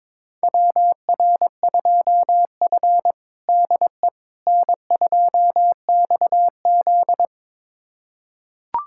Morse code, heard as a keyed tone, about 21 wpm, 700 Hz. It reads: WR2F DE N2XZ
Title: USA Callsign DE Callsign - 4 Characters - 22wpm